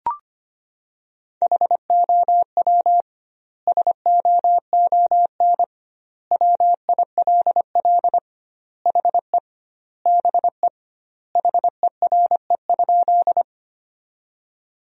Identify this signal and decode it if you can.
Morse code, heard as a keyed tone, about 25 wpm, 700 Hz. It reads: HOW SOON WILL HE BE HERE?
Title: 2x Word-Spacing - Sentences from Top 400 Words - 25wpm